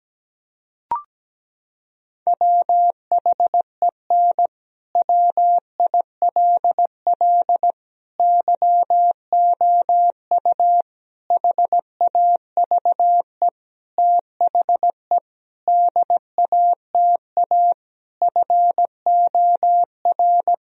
Morse code, heard as a keyed tone, about 17 wpm, 700 Hz. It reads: WHEN WILL YOU HAVE THE DATA FOR
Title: Sentences from Top 1000 Words - Review Part 3 - 17wpm